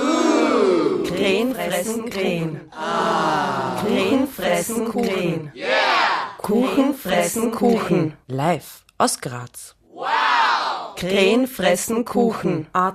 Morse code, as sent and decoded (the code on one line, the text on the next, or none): none